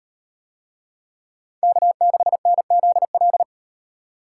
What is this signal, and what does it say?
Morse code, heard as a keyed tone, about 36 wpm, 700 Hz. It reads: K6NZL